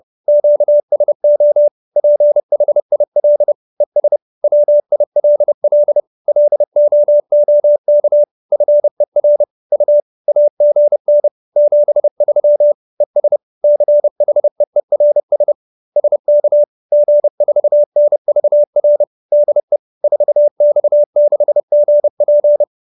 Morse code, heard as a keyed tone, about 30 wpm, 590 Hz. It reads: QSO PHIL ES WILL LOOK FER U AGN 73 ES CHEERS SK G4NVR DE 4X6GP
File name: QSO315 30